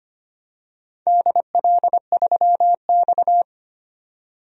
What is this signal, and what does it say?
Morse code, heard as a keyed tone, about 25 wpm, 700 Hz. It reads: DL3X
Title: DXCC-100 Call Signs - 4 Characters - 25wpm